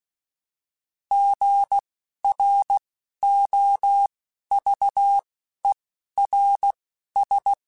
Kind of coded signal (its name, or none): Morse code